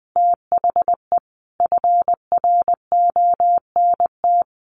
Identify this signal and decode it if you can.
Morse code, heard as a keyed tone, about 20 wpm, 700 Hz. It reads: THE FRONT
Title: Sentences from Top 1400 Words 20wpm